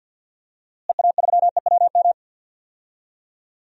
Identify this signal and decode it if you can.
Morse code, heard as a keyed tone, about 48 wpm, 700 Hz. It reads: EA3EWM
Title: DXCC-100 Call Signs - 6 Characters - Repeated 3x - 50wpm